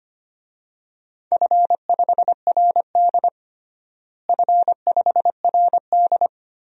Morse code, heard as a keyed tone, about 25 wpm, 700 Hz. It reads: F5RD F5RD
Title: DXCC-100 Call Signs - 4 Characters - Repeated 2x - 25wpm